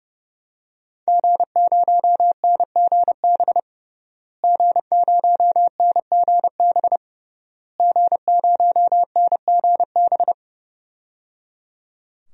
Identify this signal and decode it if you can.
Morse code, heard as a keyed tone, about 30 wpm, 700 Hz. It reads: G0NGB G0NGB G0NGB
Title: DXCC-100 Call Signs - 5 Characters - Repeated 3x - 30wpm